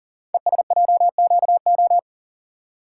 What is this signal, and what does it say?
Morse code, heard as a keyed tone, about 42 wpm, 700 Hz. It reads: ESJQO
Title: Code Groups - 5 Letters - 40wpm